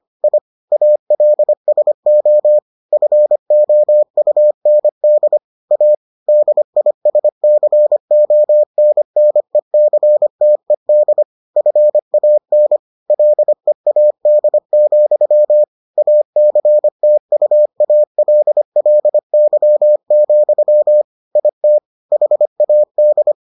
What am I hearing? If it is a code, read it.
Morse code, heard as a keyed tone, about 25 wpm, 600 Hz. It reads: I ALSO FOUND A DISCONNECTED FAN LEAD, ACTUALLY, IT HAD